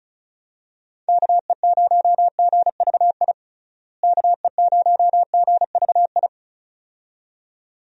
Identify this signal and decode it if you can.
Morse code, heard as a keyed tone, about 36 wpm, 700 Hz. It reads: KE0GVI KE0GVI
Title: USA Call Signs - 6 Characters - Repeated 2x - 35wpm